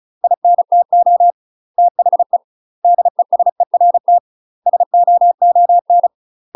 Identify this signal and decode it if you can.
Morse code, heard as a keyed tone, about 35 wpm, 700 Hz. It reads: INTO THE DESERT SOON
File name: UK MASTER QSOs TRAINING/35WPM